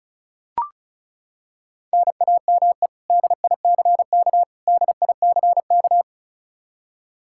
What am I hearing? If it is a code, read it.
Morse code, heard as a keyed tone, about 34 wpm, 700 Hz. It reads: NAME DICK DICK
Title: Practice QSO - Exchange 1 - Beginning - 35wpm